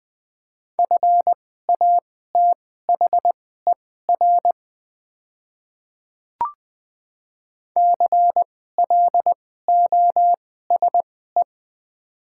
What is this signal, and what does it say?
Morse code, heard as a keyed tone, about 21 wpm, spaced as for 15 wpm, 700 Hz. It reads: FATHER CLOSE
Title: 2x Character Spacing - Top 300 Words - 20wpm